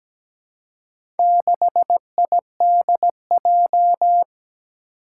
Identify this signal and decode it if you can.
Morse code, heard as a keyed tone, about 17 wpm, 700 Hz. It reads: 6IDJ
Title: Four Challenging Letter-Number Combinations 17wpm